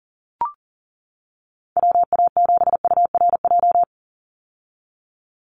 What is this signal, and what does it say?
Morse code, heard as a keyed tone, about 40 wpm, 700 Hz. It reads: WA7URJ
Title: USA Call Signs 40wpm